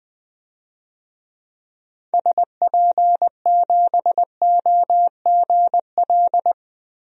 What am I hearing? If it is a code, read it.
Morse code, heard as a keyed tone, about 20 wpm, 700 Hz. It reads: SP7OGL
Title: DXCC-100 Call Signs - 6 Characters - Repeated 2x - 20wpm